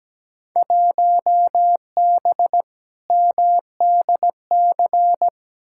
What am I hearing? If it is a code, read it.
Morse code, heard as a keyed tone, about 17 wpm, 700 Hz. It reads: 1B MDC
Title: Field Day Exchanges Encore - 17wpm